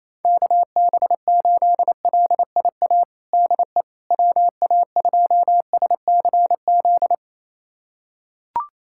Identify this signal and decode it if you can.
Morse code, heard as a keyed tone, about 28 wpm, 700 Hz. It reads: KB8LIA DE WA2SCZ
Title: USA Callsign DE Callsign - 6 Characters - 28wpm